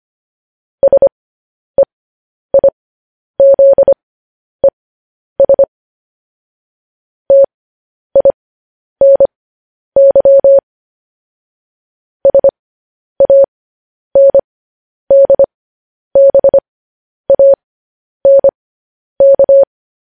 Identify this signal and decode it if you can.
Morse code, heard as a keyed tone, about 25 wpm, spaced as for 10 wpm, 550 Hz. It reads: SEIZES TINY SANDBANK
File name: podcast / News headlines-1x-25x10